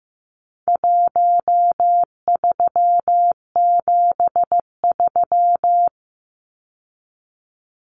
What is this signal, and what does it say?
Morse code, heard as a keyed tone, about 15 wpm, 700 Hz. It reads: 1373
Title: Numbers - Contest Style - 4-Digit 15wpm